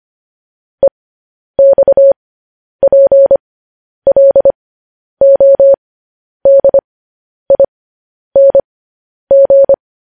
Morse code, heard as a keyed tone, about 25 wpm, spaced as for 10 wpm, 550 Hz. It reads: EXPLODING
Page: news headlines in morse code at 10 wpm